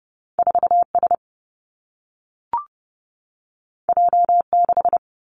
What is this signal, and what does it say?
Morse code, heard as a keyed tone, about 30 wpm, 700 Hz. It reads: 4S J6
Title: Two Challenging Letter-Number Combinations 30wpm